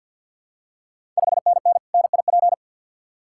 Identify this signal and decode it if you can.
Morse code, heard as a keyed tone, about 51 wpm, 690 Hz. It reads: 5NN NIP
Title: Ontario QSO Party Exchange 50wpm